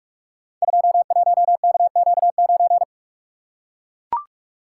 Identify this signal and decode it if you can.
Morse code, heard as a keyed tone, about 44 wpm, 700 Hz. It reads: 21KQ9
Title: Code Groups - 5 Letter:Numbers - 45wpm